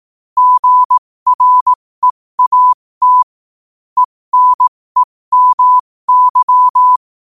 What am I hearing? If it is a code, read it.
Morse code, heard as a keyed tone, about 18 wpm, spaced as for 15 wpm, 1000 Hz. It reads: GREAT ENEMY
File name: 15wpm